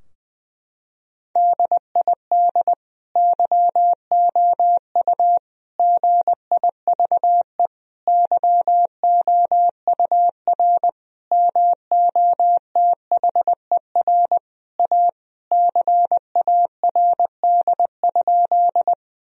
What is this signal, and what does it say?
Morse code, heard as a keyed tone, about 21 wpm, 700 Hz. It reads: DID YOU GIVE YOUR MOTHER A CARD?
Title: Sentences from Top 1000 Words - Review Part 4 - 20wpm